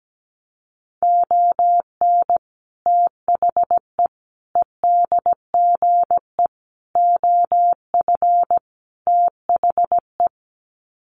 Morse code, heard as a keyed tone, about 17 wpm, 700 Hz. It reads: ON THE EDGE OF THE
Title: Sets of 5 Words 17wpm